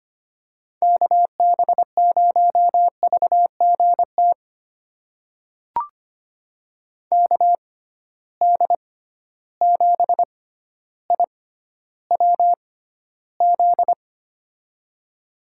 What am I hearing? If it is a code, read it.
Morse code, heard as a keyed tone, about 25 wpm, spaced as for 13 wpm, 700 Hz. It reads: KB0VGT KD7IWZ
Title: USA Call Signs - 6 Characters - 6x Character Spacing - 25wpm